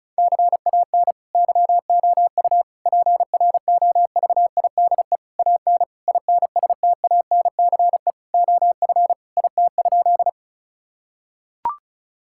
Morse code, heard as a keyed tone, about 36 wpm, 700 Hz. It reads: CAN YOU PROVIDE AN INSTANCE OF IT?
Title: Sentences from Top 1000 Words - Review Part 5 - 35wpm